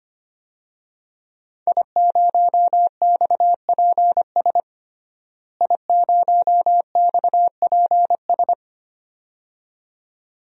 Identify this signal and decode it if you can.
Morse code, heard as a keyed tone, about 25 wpm, 700 Hz. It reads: I0XPS I0XPS
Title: DXCC-100 Call Signs - 5 Characters - Repeated 2x - 25wpm